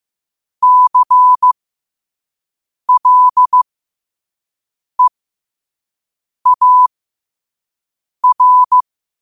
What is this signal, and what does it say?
Morse code, heard as a keyed tone, about 15 wpm, spaced as for 5 wpm, 1000 Hz. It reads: CLEAR
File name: qotdfiles / 5wpm